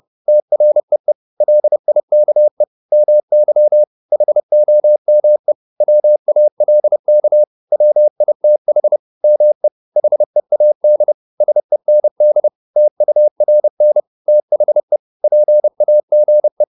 Morse code, heard as a keyed tone, about 30 wpm, 600 Hz. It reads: TREE LIKE MY HOME WALK WITH ME HEAD SEND TURN THE PAGE